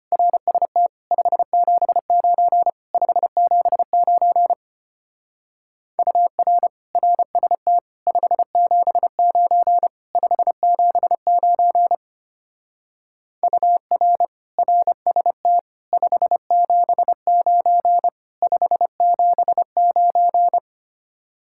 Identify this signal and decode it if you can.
Morse code, heard as a keyed tone, about 30 wpm, 700 Hz. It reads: RST 579 579 UR RST 579 579 UR RST 579 579
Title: Speed-Racing Common QSO Phrases Encore - 25wpm